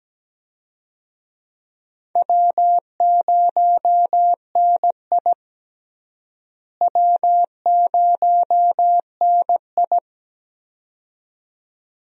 Morse code, heard as a keyed tone, about 17 wpm, 700 Hz. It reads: W0NI W0NI
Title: USA Call Signs - 4 Characters - Repeated 2x - 17wpm